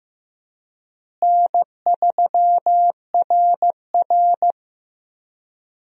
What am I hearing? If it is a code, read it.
Morse code, heard as a keyed tone, about 16 wpm, 700 Hz. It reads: N3RR
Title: USA Call Signs - 4 Characters - Repeated 3x - 15wpm